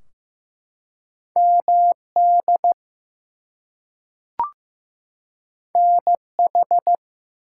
Morse code, heard as a keyed tone, about 15 wpm, 700 Hz. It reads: MD NH
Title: US States and Canadian Provinces 15wpm